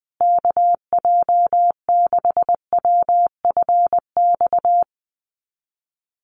Morse code, heard as a keyed tone, about 20 wpm, 700 Hz. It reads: KJ6WFX
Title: USA Call Signs Repeated 3x 20wpm